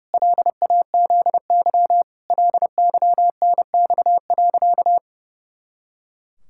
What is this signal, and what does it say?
Morse code, heard as a keyed tone, about 30 wpm, 700 Hz. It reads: LAZY LYNX.
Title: Pangram Challenge v2 - 30wpm